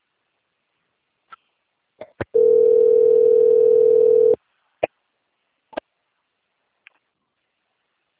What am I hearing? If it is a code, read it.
Ringback tone (North America), 440 + 480 Hz.